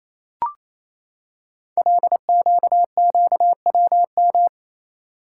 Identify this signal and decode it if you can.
Morse code, heard as a keyed tone, about 27 wpm, 700 Hz. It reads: LQQWM
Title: Code Groups - 5 Letters - 28wpm